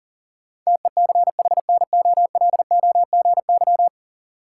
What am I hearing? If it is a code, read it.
Morse code, heard as a keyed tone, about 37 wpm, 700 Hz. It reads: TECHNOLOGY
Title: Sentences from Top 1000 Words - Review Part 5 - New Word Drill - 40wpm